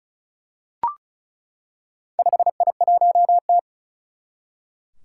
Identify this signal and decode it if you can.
Morse code, heard as a keyed tone, about 36 wpm, 700 Hz. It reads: 5I1T
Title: Four Challenging Letter-Number Combinations 35wpm